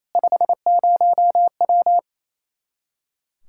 Morse code, heard as a keyed tone, about 28 wpm, 700 Hz. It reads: 50W
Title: Practice QSO - Exchange 2 - Info - 28wpm